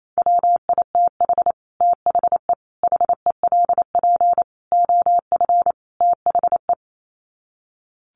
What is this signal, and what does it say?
Morse code, heard as a keyed tone, about 28 wpm, 700 Hz. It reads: WITH THE HELP OF THE
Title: Sets of 5 Words Encore 28wpm